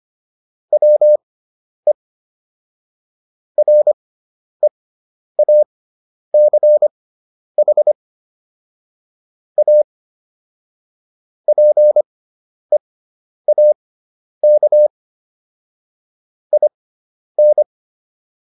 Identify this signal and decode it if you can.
Morse code, heard as a keyed tone, about 25 wpm, spaced as for 10 wpm, 600 Hz. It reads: WE REACH A PEAK IN